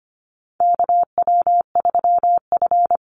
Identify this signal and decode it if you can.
Morse code, heard as a keyed tone, about 25 wpm, 700 Hz. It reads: KW3F
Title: Call Signs - 25wpm